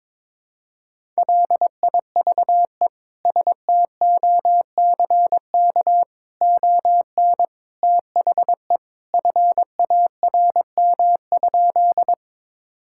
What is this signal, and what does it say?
Morse code, heard as a keyed tone, about 22 wpm, 700 Hz. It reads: LIVE STOCK ON THE FARM?
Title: Sentences from Top 1000 Words - Review Part 5 - 22wpm